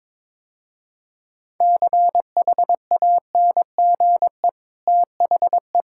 Morse code, heard as a keyed tone, about 22 wpm, 690 Hz. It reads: CHANGE THE
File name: practice / 22wpm / Easy Sentences with Hints - 22wpm